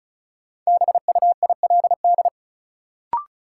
Morse code, heard as a keyed tone, about 36 wpm, 700 Hz. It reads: BUILD
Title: Sentences from Top 1000 Words - Review Part 1 - New Word Drill - 35wpm